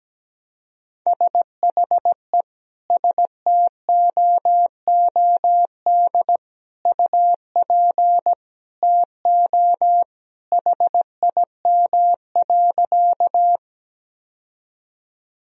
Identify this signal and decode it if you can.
Morse code, heard as a keyed tone, about 17 wpm, 700 Hz. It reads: SHE STOOD UP TO HIM.
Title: Sentences from Top 1000 Words - Review Part 2 - 17wpm